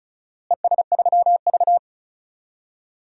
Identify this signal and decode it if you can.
Morse code, heard as a keyed tone, about 34 wpm, 700 Hz. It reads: ES3V
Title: DXCC-100 Call Signs - 4 Characters - Repeated 2x - 35wpm